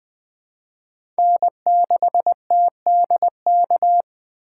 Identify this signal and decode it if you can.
Morse code, heard as a keyed tone, about 20 wpm, 700 Hz. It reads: N6TDK
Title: USA Call Signs - 4x Character Spacing - 20wpm